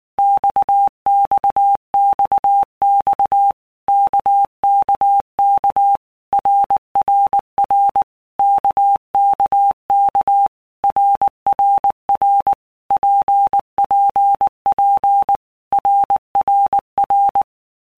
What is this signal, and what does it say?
Morse code, heard as a keyed tone, about 19 wpm, 800 Hz. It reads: XXXX KKK RRR KKK RRR PPP RRR
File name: All Groups 1-minute Blocks / Groups@18 / RKPX